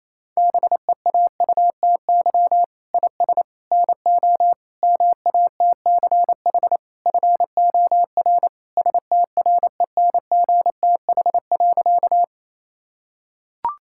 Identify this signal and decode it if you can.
Morse code, heard as a keyed tone, about 27 wpm, 700 Hz. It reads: BEAUTY IS NO MATCH FOR STRENGTH.
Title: Sentences from Top 1600 Words 28wpm